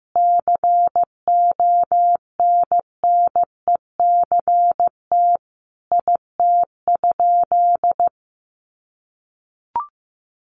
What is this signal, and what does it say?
Morse code, heard as a keyed tone, about 15 wpm, 700 Hz. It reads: CONNECT IT?
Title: Sentences from Top 1500 Words 15wpm